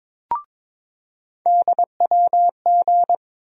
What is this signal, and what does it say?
Morse code, heard as a keyed tone, about 21 wpm, 700 Hz. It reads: DWG
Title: USA Call Sign Postfixes - 3 Letters 22wpm